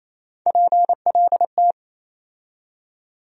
Morse code, heard as a keyed tone, about 28 wpm, 700 Hz. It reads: PLT
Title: USA Call Sign Postfixes - 3 Letters 28wpm